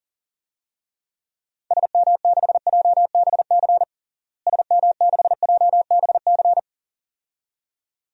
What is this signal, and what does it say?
Morse code, heard as a keyed tone, about 40 wpm, 700 Hz. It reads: SM6JBC SM6JBC
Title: DXCC-100 Call Signs - 6 Characters - Repeated 2x - 40wpm